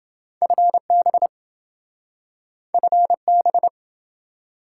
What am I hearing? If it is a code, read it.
Morse code, heard as a keyed tone, about 26 wpm, 700 Hz. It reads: FB FB